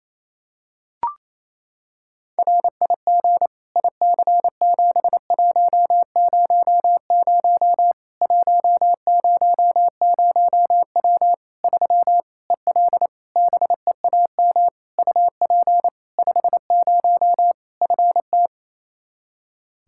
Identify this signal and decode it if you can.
Morse code, heard as a keyed tone, about 28 wpm, 700 Hz. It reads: RIG IC7100 100W 3 EL BEAM UP 50 FT
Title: Practice QSO - Exchange 2 - Info - 28wpm